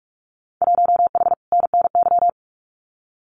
Morse code, heard as a keyed tone, about 44 wpm, 700 Hz. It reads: 1H NNY
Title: Winter Field Day Exchanges 45wpm